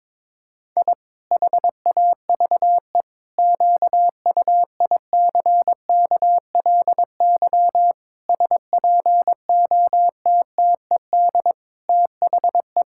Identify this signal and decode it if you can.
Morse code, heard as a keyed tone, about 22 wpm, 700 Hz. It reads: I HAVE QUICKLY SPOTTED THE
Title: Pangram Challenge v2 - 22wpm